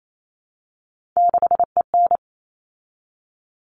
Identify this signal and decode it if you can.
Morse code, heard as a keyed tone, about 28 wpm, 700 Hz. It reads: 6EN
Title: All Cut Numbers - 3-Digit 28wpm